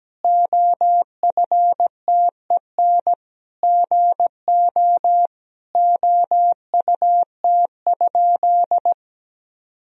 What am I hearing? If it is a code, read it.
Morse code, heard as a keyed tone, about 17 wpm, 700 Hz. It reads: OFTEN GO OUT?
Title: Sentences from Top 1000 Words - Review Part 1 - 17wpm